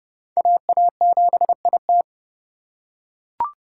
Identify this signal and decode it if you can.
Morse code, heard as a keyed tone, about 28 wpm, 700 Hz. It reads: AA7IT